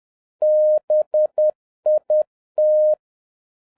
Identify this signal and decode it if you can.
Morse code, heard as a keyed tone, about 10 wpm, 610 Hz. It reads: BIT